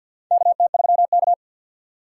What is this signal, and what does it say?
Morse code, heard as a keyed tone, about 52 wpm, 700 Hz. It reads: KT3K
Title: USA Call Signs - 4 Characters - 6x Character Spacing - 50wpm